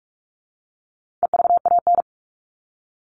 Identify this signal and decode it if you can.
Morse code, heard as a keyed tone, about 44 wpm, 700 Hz. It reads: EVAN
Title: All Cut Numbers - 4-Digit 45wpm